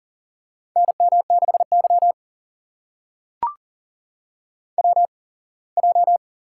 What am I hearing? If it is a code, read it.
Morse code, heard as a keyed tone, about 43 wpm, spaced as for 24 wpm, 700 Hz. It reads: NM6Y WJ
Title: USA Call Signs - 4 Characters - 8x Character Spacing - 40wpm